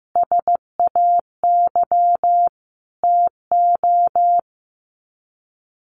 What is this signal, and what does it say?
Morse code, heard as a keyed tone, about 15 wpm, 700 Hz. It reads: SAY TO
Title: Sets of 5 Words 15wpm